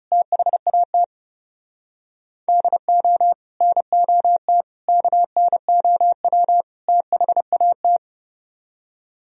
Morse code, heard as a keyed tone, about 37 wpm, 700 Hz. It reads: THAT DO NOT KNOW THAT
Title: Speed-Racing - Sets of 4 Words 30wpm